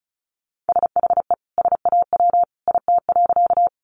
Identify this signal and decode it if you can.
Morse code, heard as a keyed tone, about 34 wpm, 700 Hz. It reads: SHE SAW IT.
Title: Sentences from Top 300 Words 35wpm